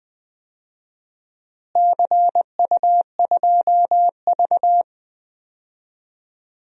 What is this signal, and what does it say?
Morse code, heard as a keyed tone, about 20 wpm, 700 Hz. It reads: CU2V
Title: DXCC-100 Call Signs - 4 Characters - Repeated 2x - 20wpm